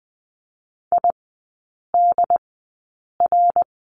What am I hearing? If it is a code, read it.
Morse code, heard as a keyed tone, about 20 wpm, spaced as for 8 wpm, 700 Hz. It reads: IDR
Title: 067 Lesson 12e - Letter d - Three Characters 20wpm